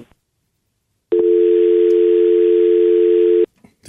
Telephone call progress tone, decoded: dial tone